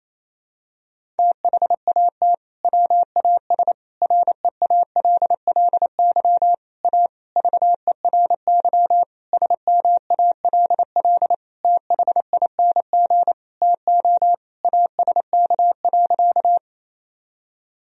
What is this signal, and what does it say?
Morse code, heard as a keyed tone, about 28 wpm, 700 Hz. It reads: THAT WAS REALLY A VERY SMALL THING TO ASK.
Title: Sentences from Top 1000 Words - Review Part 1 - 28wpm